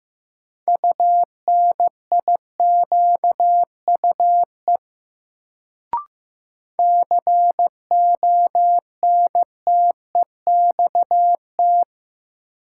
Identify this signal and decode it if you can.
Morse code, heard as a keyed tone, about 15 wpm, 700 Hz. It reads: UNIQUE CONTEXT
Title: Sentences from Top 1000 Words - Review Part 7 - New Word Drill - 15wpm